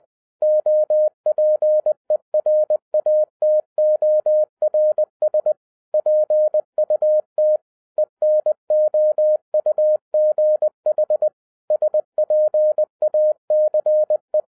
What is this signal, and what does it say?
Morse code, heard as a keyed tone, about 20 wpm, 610 Hz. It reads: OPERATORS PUT ENOUGH SPACE
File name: 2011-10-mp3 / n13-TIPS FROM MY 34 YEARS OF EXPERIENCE 20wpm